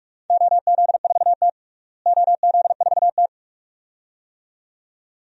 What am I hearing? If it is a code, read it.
Morse code, heard as a keyed tone, about 46 wpm, 700 Hz. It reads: OZ4T OZ4T
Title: DXCC-100 Call Signs - 4 Characters - Repeated 2x - 45wpm